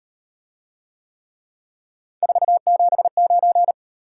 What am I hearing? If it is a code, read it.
Morse code, heard as a keyed tone, about 36 wpm, 700 Hz. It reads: 479